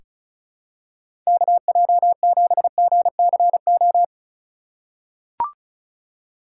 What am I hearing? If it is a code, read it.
Morse code, heard as a keyed tone, about 34 wpm, 700 Hz. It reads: KJ7GCO
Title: USA Call Signs - 4x Character Spacing - 35wpm